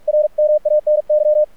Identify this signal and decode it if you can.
Morse code, heard as a keyed tone, about 48 wpm, 600 Hz, with radio noise. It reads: RMANY
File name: archived / CRSnet